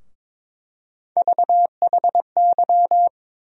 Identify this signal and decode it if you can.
Morse code, heard as a keyed tone, about 22 wpm, 700 Hz. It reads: VHY
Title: USA Call Sign Postfixes - 3 Letters 22wpm